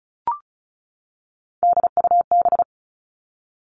Morse code, heard as a keyed tone, about 36 wpm, 700 Hz. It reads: DUB
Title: Three Challenging Letter-Number Combinations 35wpm